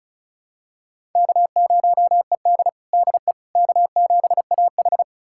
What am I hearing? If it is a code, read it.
Morse code, heard as a keyed tone, about 34 wpm, 700 Hz. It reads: K0ED DE K7AH
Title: USA Callsign DE Callsign - 4 Characters - 35wpm